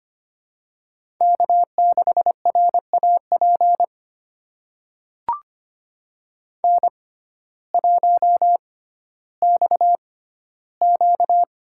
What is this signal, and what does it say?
Morse code, heard as a keyed tone, about 25 wpm, spaced as for 14 wpm, 700 Hz. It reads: K6RAP N1XQ
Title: USA Call Signs - 6x Character Spacing - 25wpm